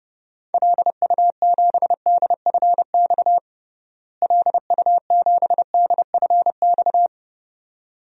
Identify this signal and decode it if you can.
Morse code, heard as a keyed tone, about 30 wpm, 700 Hz. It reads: LU7DFX LU7DFX
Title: DXCC-100 Call Signs - 6 Characters - Repeated 2x - 30wpm